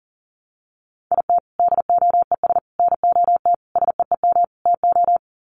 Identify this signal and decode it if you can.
Morse code, heard as a keyed tone, about 40 wpm, 700 Hz. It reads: IT DOES NOT SEEM TO
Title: Sets of 5 Words Encore2 40wpm